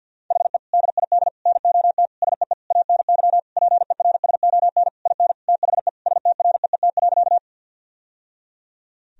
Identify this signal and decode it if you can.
Morse code, heard as a keyed tone, about 47 wpm, 700 Hz. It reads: HE DID NOT SEE ANY PERSON IN THE STREET.
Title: Sentences from Top 1000 Words - Review Part 3 - 50wpm